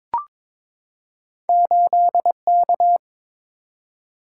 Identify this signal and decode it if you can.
Morse code, heard as a keyed tone, about 22 wpm, 700 Hz. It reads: 8K
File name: practice / 22wpm / Two Letter-Number Combinations - 22wpm